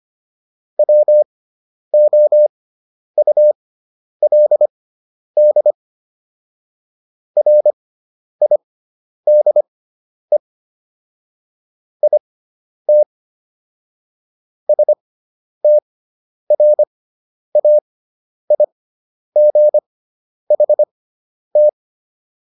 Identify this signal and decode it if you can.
Morse code, heard as a keyed tone, about 25 wpm, spaced as for 10 wpm, 600 Hz. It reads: WOULD RIDE IT STRAIGHT